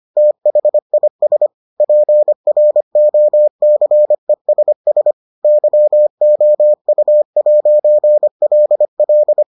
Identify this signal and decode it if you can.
Morse code, heard as a keyed tone, about 25 wpm, 600 Hz. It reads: THIS PROCESS YOU'LL